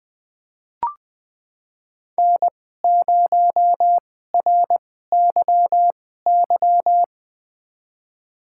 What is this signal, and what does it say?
Morse code, heard as a keyed tone, about 21 wpm, spaced as for 15 wpm, 700 Hz. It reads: N0RYY
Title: USA Call Signs - 5 Characters - 2x Character Spacing - 20wpm